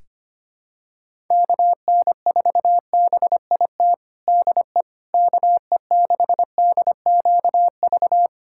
Morse code, heard as a keyed tone, about 25 wpm, 700 Hz. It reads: KN4BIT DE KE6DQV
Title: USA Callsign DE Callsign - 6 Characters - 25wpm